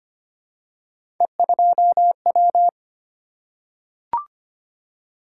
Morse code, heard as a keyed tone, about 25 wpm, 700 Hz. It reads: E2W
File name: Three Letter-Number Combinations - 25wpm